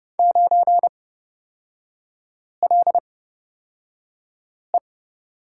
Morse code, heard as a keyed tone, about 30 wpm, spaced as for 5 wpm, 700 Hz. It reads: 9LE
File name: Lesson 38e - Letter q - Three Characters 30wpm